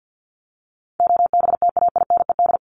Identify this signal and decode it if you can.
Morse code, heard as a keyed tone, about 50 wpm, 700 Hz. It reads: OBTAINED
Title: ED Suffix - Top 100 Words with Base Words 50wpm